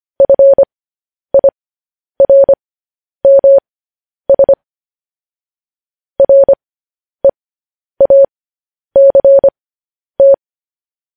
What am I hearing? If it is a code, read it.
Morse code, heard as a keyed tone, about 25 wpm, spaced as for 10 wpm, 550 Hz. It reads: FIRMS REACT
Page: news headlines in morse code at 10 wpm